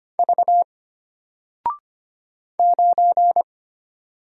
Morse code, heard as a keyed tone, about 25 wpm, 700 Hz. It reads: V 9
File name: Single-Letter-Number-25wpm